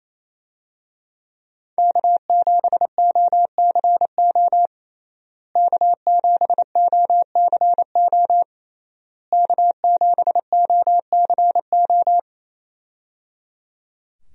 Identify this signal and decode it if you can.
Morse code, heard as a keyed tone, about 28 wpm, 700 Hz. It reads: K7OCO K7OCO K7OCO
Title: USA Call Signs - 5 Characters - Repeated 3x - 28wpm